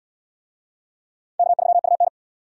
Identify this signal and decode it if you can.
Morse code, heard as a keyed tone, about 72 wpm, 700 Hz. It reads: K3UN